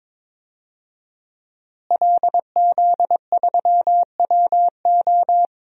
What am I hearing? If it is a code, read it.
Morse code, heard as a keyed tone, about 22 wpm, 700 Hz. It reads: LZ3WO
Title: DXCC-100 Call Signs - 5 Characters - Repeated 2x - 22wpm